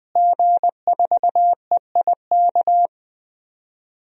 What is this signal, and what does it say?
Morse code, heard as a keyed tone, about 20 wpm, 700 Hz. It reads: G4EIK
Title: DXCC-100 Call Signs - 5 Characters - Repeated 2x - 20wpm